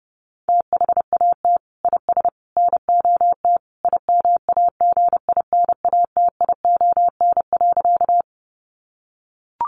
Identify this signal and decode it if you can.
Morse code, heard as a keyed tone, about 30 wpm, 700 Hz. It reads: THAT IS NOT IMAGINATION.
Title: Sentences from Top 1300 Words 30wpm